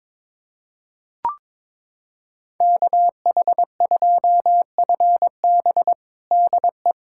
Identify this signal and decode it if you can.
Morse code, heard as a keyed tone, about 22 wpm, 700 Hz. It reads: KH2FB DE
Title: USA Callsign DE Callsign - 5 Characters - 22wpm